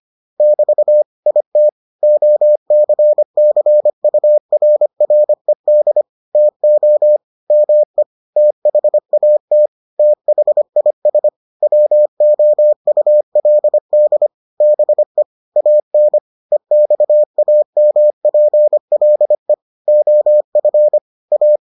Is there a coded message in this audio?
Yes: Morse code from 0.4 to 21.6 s, heard as a keyed tone, about 25 wpm, 600 Hz. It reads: <BT> IT OCCURRED TO ME THAT THIS WOULD BE AN EXAMPLE OF A